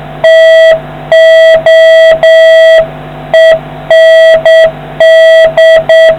Morse code, heard as a keyed tone, about 8 wpm, 650 Hz, with noise. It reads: TO END